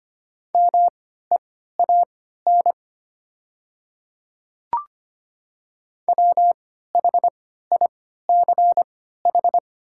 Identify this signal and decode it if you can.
Morse code, heard as a keyed tone, about 25 wpm, spaced as for 14 wpm, 700 Hz. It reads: MEAN WHICH
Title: 3x Character Spacing - Top 500 Words - 25wpm